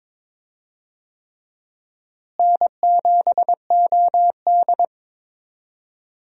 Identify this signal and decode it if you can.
Morse code, heard as a keyed tone, about 22 wpm, 700 Hz. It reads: N7OD
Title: USA Call Signs - 4 Characters - Repeated 3x - 22wpm